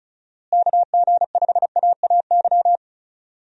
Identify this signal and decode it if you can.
Morse code, heard as a keyed tone, about 34 wpm, 700 Hz. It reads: KG5AAY
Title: USA Call Signs - 6 Characters - Repeated 3x - 35wpm